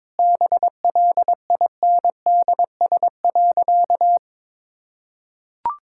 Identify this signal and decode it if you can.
Morse code, heard as a keyed tone, about 22 wpm, 690 Hz. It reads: BLINDS.
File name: practice / 22wpm / Easy Sentences with Hints - 22wpm